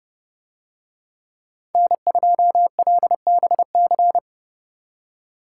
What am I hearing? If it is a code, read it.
Morse code, heard as a keyed tone, about 28 wpm, 700 Hz. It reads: N2LBC